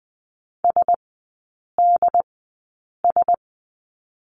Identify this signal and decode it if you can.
Morse code, heard as a keyed tone, about 20 wpm, spaced as for 8 wpm, 700 Hz. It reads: SDS
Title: 067 Lesson 12e - Letter d - Three Characters 20wpm